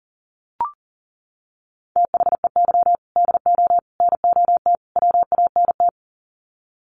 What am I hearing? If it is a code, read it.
Morse code, heard as a keyed tone, about 40 wpm, 700 Hz. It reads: THEY DO NOT WANT
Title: Sets of 4 Words 40wpm